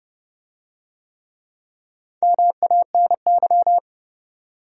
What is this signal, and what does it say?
Morse code, heard as a keyed tone, about 28 wpm, 700 Hz. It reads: MANY